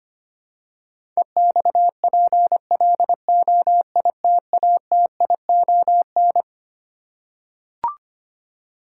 Morse code, heard as a keyed tone, about 25 wpm, 700 Hz. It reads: EXPLOITATION
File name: Twelve Letter Words - 25wpm